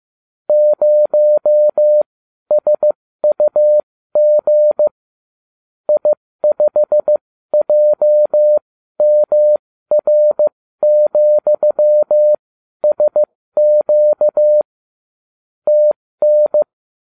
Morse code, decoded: 0SUG I5JMR,SQ TN

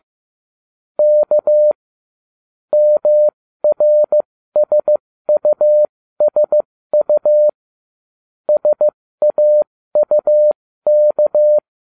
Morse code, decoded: K MRSUSU SAUK

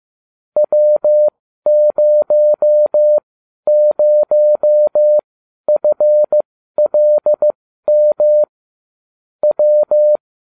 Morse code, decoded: W00FLM W

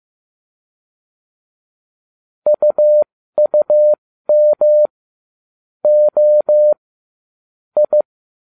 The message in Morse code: UUM O I

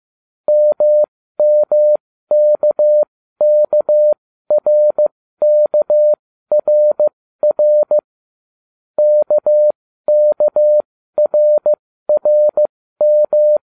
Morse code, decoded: MMKKRKRR KKRRM

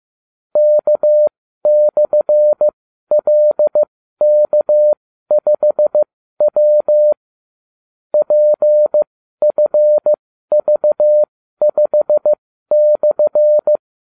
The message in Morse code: K/LK5W PFV5/

